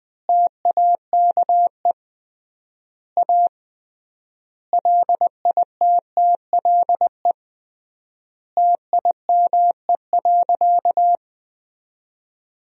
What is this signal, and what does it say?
Morse code, heard as a keyed tone, about 20 wpm, 700 Hz. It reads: TAKE A LITTLE TIME.
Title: 3x Word-Spacing - Sentences from Top 200 Words - 20wpm